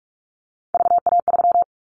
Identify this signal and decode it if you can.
Morse code, heard as a keyed tone, about 44 wpm, 700 Hz. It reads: VA3
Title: Callsign Prefixes for Top 100 DXCC - 45wpm